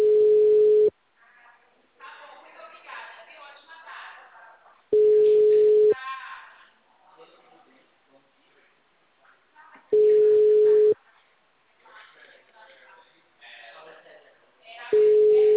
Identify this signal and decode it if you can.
Ringback tone (Europe, 425 Hz), with speech.